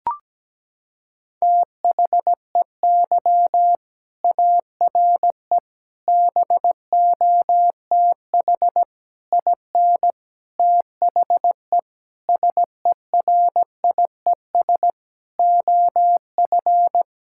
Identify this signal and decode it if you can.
Morse code, heard as a keyed tone, about 17 wpm, 700 Hz. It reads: THEY ARE BOTH IN THE SERIES OF
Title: Sentences from Top 1000 Words - Review Part 5 - 17wpm